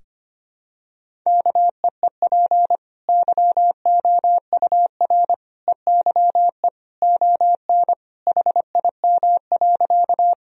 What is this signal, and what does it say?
Morse code, heard as a keyed tone, about 25 wpm, 700 Hz. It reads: KEEP YOUR EYE ON HIM.
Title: Sentences from Top 1000 Words - Review Part 1 - 25wpm